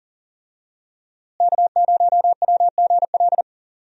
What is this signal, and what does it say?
Morse code, heard as a keyed tone, about 37 wpm, 700 Hz. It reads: K0WGL
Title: USA Call Signs - 5 Characters - Repeated 3x - 40wpm